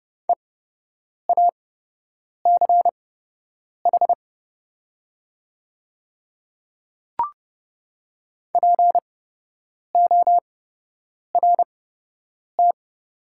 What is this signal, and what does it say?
Morse code, heard as a keyed tone, about 31 wpm, spaced as for 8 wpm, 700 Hz. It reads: EACH PORT
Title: 8x Character Spacing - Top 200 Words - 30wpm